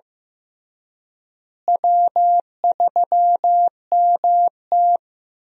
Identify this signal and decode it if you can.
Morse code, heard as a keyed tone, about 15 wpm, 700 Hz. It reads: W3MT
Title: USA Call Signs - 4 Characters - Repeated 3x - 15wpm